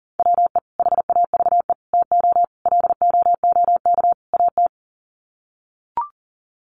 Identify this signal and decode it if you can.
Morse code, heard as a keyed tone, about 40 wpm, 700 Hz. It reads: WE HAVE TO LOOK AT
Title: Sets of 5 Words Encore2 40wpm